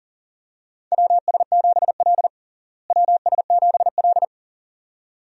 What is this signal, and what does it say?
Morse code, heard as a keyed tone, about 40 wpm, 700 Hz. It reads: WS7L WS7L
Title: USA Call Signs - 4 Characters - Repeated 2x - 40wpm